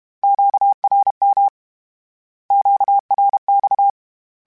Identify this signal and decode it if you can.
Morse code, heard as a keyed tone, about 32 wpm, 800 Hz. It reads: QRM QRX
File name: Q Calls / Q Calls-mp3s / Q Calls Twice @30